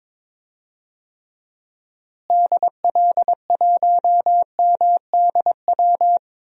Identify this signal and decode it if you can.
Morse code, heard as a keyed tone, about 22 wpm, 700 Hz. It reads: DL1MDW
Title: DXCC-100 Call Signs - 6 Characters - Repeated 2x - 22wpm